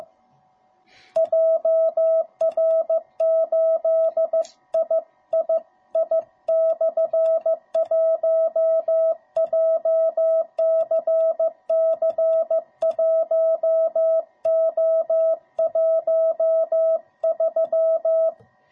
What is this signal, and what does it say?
Morse code, heard as a keyed tone, about 15 wpm, 660 Hz. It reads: JR8III/1 JCC1O13